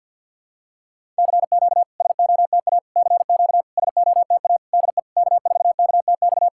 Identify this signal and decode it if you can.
Morse code, heard as a keyed tone, about 51 wpm, 690 Hz. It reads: CQ SOTA CQ SOTA DE K4KTX
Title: SOTA QSO Practice 50wpm